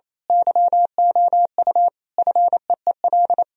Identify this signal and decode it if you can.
Morse code, heard as a keyed tone, about 26 wpm, 700 Hz. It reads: YOU FEEL
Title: Speed-Racing - Sets of 5 Words 28wpm